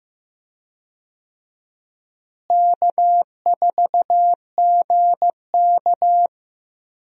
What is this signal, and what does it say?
Morse code, heard as a keyed tone, about 15 wpm, 700 Hz. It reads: K4GK
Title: USA Call Signs - 4 Characters - Repeated 3x - 15wpm